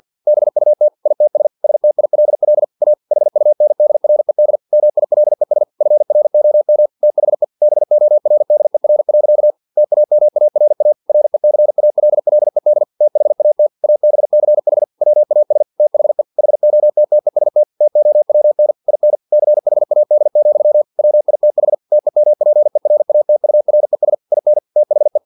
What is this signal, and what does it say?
Morse code, heard as a keyed tone, about 50 wpm, 600 Hz. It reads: BUT ITS STILL A HUNDRED MILES FROM THE BORDER. TAMARA RECALLED THAT ABCH WAS THE HOTTEST TOWN IN CHAD, WITH TEMPERATURES IN THE
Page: Never 49wpm